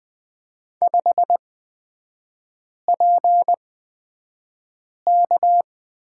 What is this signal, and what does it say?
Morse code, heard as a keyed tone, about 20 wpm, spaced as for 5 wpm, 700 Hz. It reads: 5PK
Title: 224 Lesson 36e - BT - Three Characters 20wpm